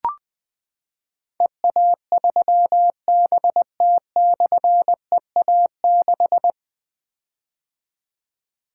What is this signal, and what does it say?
Morse code, heard as a keyed tone, about 20 wpm, 700 Hz. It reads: EA3BT/EA6
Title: Difficult Call Signs 20wpm